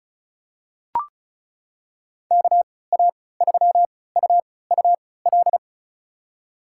Morse code, heard as a keyed tone, about 36 wpm, spaced as for 20 wpm, 700 Hz. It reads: KA3UUL
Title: USA Call Signs - 6 Characters - 3x Character Spacing - 35wpm